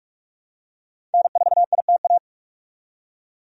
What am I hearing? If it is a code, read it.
Morse code, heard as a keyed tone, about 46 wpm, 700 Hz. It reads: N4ITA
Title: USA Call Signs - 5 Characters - 45wpm